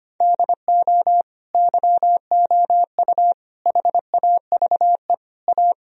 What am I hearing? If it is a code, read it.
Morse code, heard as a keyed tone, about 25 wpm, 700 Hz. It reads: DO YOU HAVE A
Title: Sentences from Top 1000 Words - Review Part 4 - 25wpm